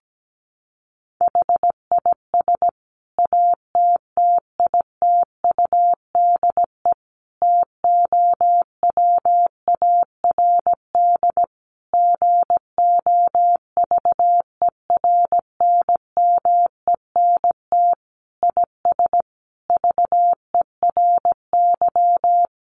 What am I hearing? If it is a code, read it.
Morse code, heard as a keyed tone, about 17 wpm, 700 Hz. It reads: HIS ATTITUDE TOWARD GOVERNMENT IS VERY
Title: Sentences from Top 1300 Words 17wpm